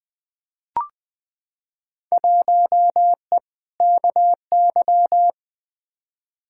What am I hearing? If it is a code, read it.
Morse code, heard as a keyed tone, about 20 wpm, 700 Hz. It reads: 1E KY
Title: Field Day Exchanges - 20wpm